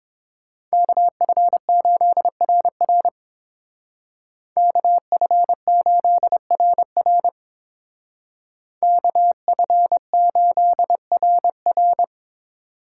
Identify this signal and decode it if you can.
Morse code, heard as a keyed tone, about 27 wpm, 700 Hz. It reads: KF8RR KF8RR KF8RR
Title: Speed Racing - USA 2x2 Call Signs 22wpm